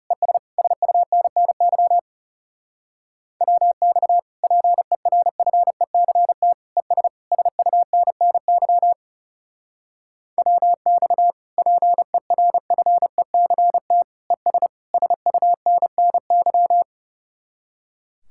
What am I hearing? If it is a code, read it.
Morse code, heard as a keyed tone, about 37 wpm, 700 Hz. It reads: ES SUNNY WX PERFECT ES SUNNY WX PERFECT ES SUNNY
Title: Speed-Racing Common QSO Phrases - 30wpm